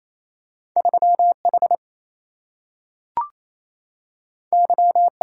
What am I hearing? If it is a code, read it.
Morse code, heard as a keyed tone, about 28 wpm, 700 Hz. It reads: 3H Y
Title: Single-Letter Single-Number 28wpm